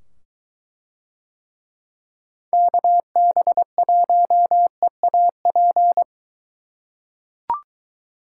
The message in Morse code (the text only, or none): KB1EAP